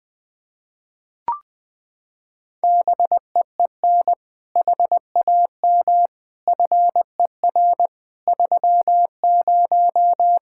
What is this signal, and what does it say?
Morse code, heard as a keyed tone, about 20 wpm, 700 Hz. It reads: BEEN HAM FER 30
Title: Practice QSO - Exchange 3 - More Info - 20wpm